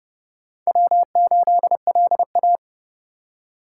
Morse code, heard as a keyed tone, about 27 wpm, 700 Hz. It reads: W8LA